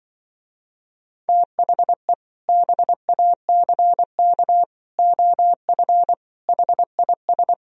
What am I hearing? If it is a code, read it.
Morse code, heard as a keyed tone, about 24 wpm, 700 Hz. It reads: THE BACK OF HIS
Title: Speed-Racing - Sets of 4 Words 20wpm